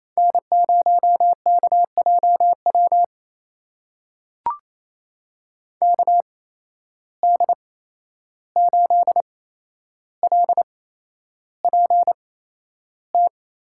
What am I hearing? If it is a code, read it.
Morse code, heard as a keyed tone, about 28 wpm, spaced as for 11 wpm, 700 Hz. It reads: N0KJW KD8LPT
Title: USA Call Signs - 8x Character Spacing - 28wpm